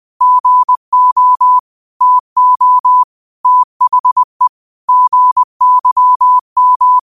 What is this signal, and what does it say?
Morse code, heard as a keyed tone, about 20 wpm, 1000 Hz. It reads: GO TO THE GYM
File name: qotdfiles / 20wpm